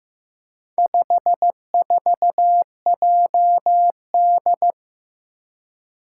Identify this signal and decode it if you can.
Morse code, heard as a keyed tone, about 15 wpm, 700 Hz. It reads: 54JD
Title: Four Challenging Letter-Number Combinations 15wpm